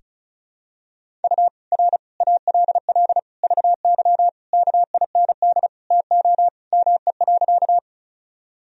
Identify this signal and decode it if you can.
Morse code, heard as a keyed tone, about 34 wpm, 700 Hz. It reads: U R ALL VY KIND TO ME.
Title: Sentences from Top 200 Words and Abbreviations 35wpm